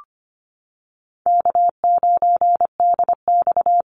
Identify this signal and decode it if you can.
Morse code, heard as a keyed tone, about 25 wpm, 700 Hz. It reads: K9DX
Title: Easy Call Signs Repeated 2x - 25wpm